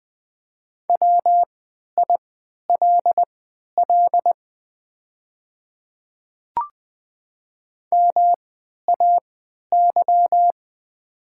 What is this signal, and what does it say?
Morse code, heard as a keyed tone, about 20 wpm, spaced as for 11 wpm, 700 Hz. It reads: WILL MAY
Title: 3x Character Spacing - Top 100 Words - 20wpm